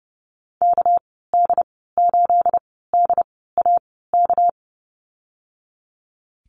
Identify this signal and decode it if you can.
Morse code, heard as a keyed tone, about 31 wpm, spaced as for 17 wpm, 700 Hz. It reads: KD8DAK
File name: Lesson 37g - Letter x - Call Signs 30wpm